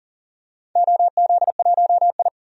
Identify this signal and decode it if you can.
Morse code, heard as a keyed tone, about 40 wpm, 700 Hz. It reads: OZ1I
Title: DXCC-100 Call Signs - 4 Characters - 40wpm